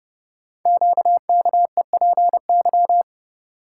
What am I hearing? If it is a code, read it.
Morse code, heard as a keyed tone, about 30 wpm, 700 Hz. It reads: QKEPY
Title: Code Groups - 5 Letters - 30wpm